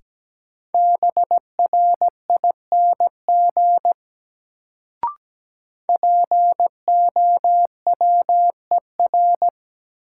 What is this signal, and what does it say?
Morse code, heard as a keyed tone, about 17 wpm, 700 Hz. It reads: BRING POWER
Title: Sentences from Top 500 Words - New Word Drill - 17wpm